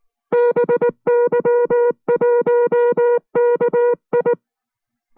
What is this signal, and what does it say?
Morse code, heard as a keyed tone, about 19 wpm, 480 Hz. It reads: BY1KI